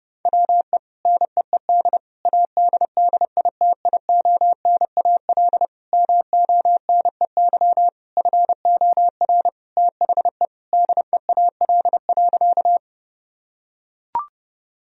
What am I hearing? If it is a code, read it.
Morse code, heard as a keyed tone, about 30 wpm, 700 Hz. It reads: WE NEED ADDITIONAL MONEY FOR THE DEAL.
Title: Sentences from Top 1000 Words - Review Part 4 - 30wpm